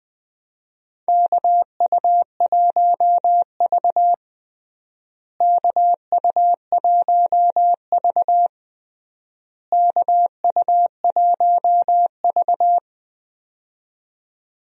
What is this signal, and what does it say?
Morse code, heard as a keyed tone, about 20 wpm, 700 Hz. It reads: KU1V KU1V KU1V
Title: USA Call Signs - 4 Characters - Repeated 3x - 20wpm